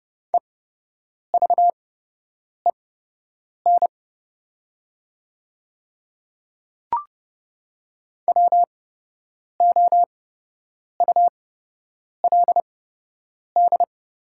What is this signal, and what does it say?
Morse code, heard as a keyed tone, about 30 wpm, spaced as for 8 wpm, 700 Hz. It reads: EVEN WOULD
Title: 8x Character Spacing - Top 200 Words - 30wpm